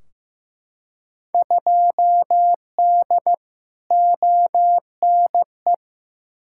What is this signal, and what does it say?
Morse code, heard as a keyed tone, about 15 wpm, 700 Hz. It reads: 2D ONE
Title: Field Day Exchanges - 15wpm